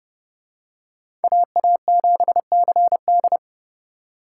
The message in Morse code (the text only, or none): AA7CD